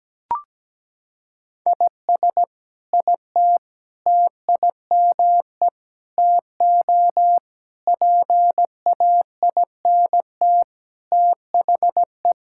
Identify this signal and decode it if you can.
Morse code, heard as a keyed tone, about 17 wpm, 700 Hz. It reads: IS IT TIME TO PAINT THE
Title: Sentences from Top 1000 Words - Review Part 6 - 17wpm